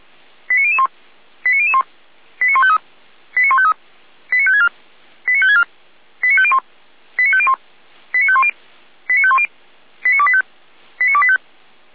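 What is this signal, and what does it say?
Five-tone selective call (ZVEI), heard as quick runs of five tones, tones of about 70 ms. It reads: 89001 89001 89153 89153 89564 89564 89501 89501 89510 89510 89285 89285
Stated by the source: ZVEI five-tone call received on an SDR.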